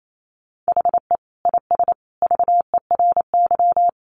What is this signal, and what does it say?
Morse code, heard as a keyed tone, about 28 wpm, 700 Hz. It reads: HE IS VERY
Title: Sentences from Top 400 Words 28wpm